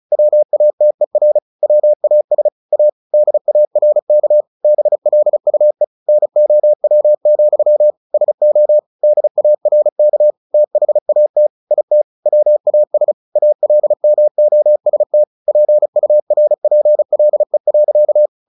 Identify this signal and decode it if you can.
Morse code, heard as a keyed tone, about 34 wpm, 600 Hz. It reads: WATER WAS A DARK BLUE NOW, SO DARK THAT IT WAS ALMOST PURPLE.
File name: the old man and the sea / the old man and the sea 1 35wpm